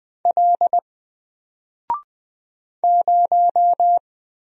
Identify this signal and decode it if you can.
Morse code, heard as a keyed tone, about 20 wpm, 700 Hz. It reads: L 0
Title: Single Character (Letter:Number:Punctuation:Prosign) 20wpm